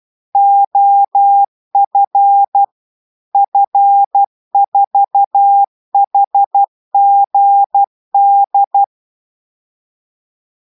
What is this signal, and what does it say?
Morse code, heard as a keyed tone, about 12 wpm, 800 Hz. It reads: OF F4HGD